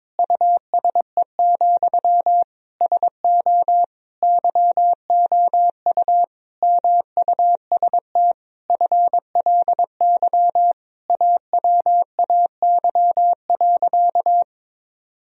Morse code, heard as a keyed tone, about 22 wpm, 700 Hz. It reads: USE, SO YOU MUST FLY AWAY.
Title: Sentences from Top 1000 Words - Review Part 2 - 22wpm